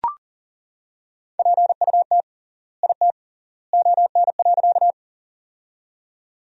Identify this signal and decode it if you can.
Morse code, heard as a keyed tone, about 40 wpm, 700 Hz. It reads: PUT IT ON.
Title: 3x Word-Spacing - Sentences from Top 200 Words - 40wpm